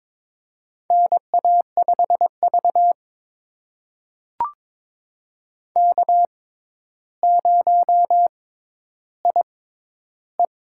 Morse code, heard as a keyed tone, about 22 wpm, spaced as for 11 wpm, 700 Hz. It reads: NA5V K0IE
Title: USA Call Signs - 6x Character Spacing - 22wpm